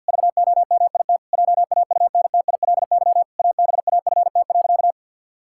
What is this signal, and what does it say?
Morse code, heard as a keyed tone, about 50 wpm, 700 Hz. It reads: VOMIT JAUNTILY ABAFT.
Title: Pangram Challenge v2 - 50wpm